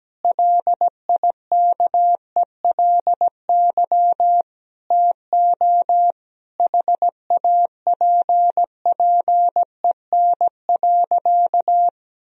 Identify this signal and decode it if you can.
Morse code, heard as a keyed tone, about 17 wpm, 700 Hz. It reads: LIKELY TO HAPPEN.
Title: Sentences from Top 1000 Words - Review Part 4 - 17wpm